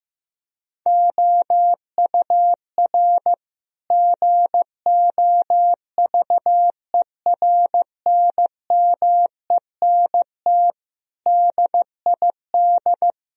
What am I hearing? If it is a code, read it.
Morse code, heard as a keyed tone, about 15 wpm, 700 Hz. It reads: OUR GOVERNMENT DID
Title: Sentences from Top 1000 Words - Review Part 3 - 15wpm